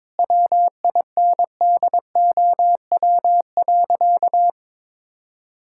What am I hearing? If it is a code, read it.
Morse code, heard as a keyed tone, about 22 wpm, 700 Hz. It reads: WINDOW.
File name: practice / 22wpm / Sentences with 2-Word Phrases - 22wpm